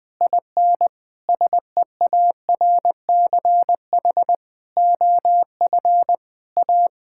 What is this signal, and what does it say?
Morse code, heard as a keyed tone, about 20 wpm, 700 Hz. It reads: IN SEARCH OF A
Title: Sentences with 8-Word Phrases - 20wpm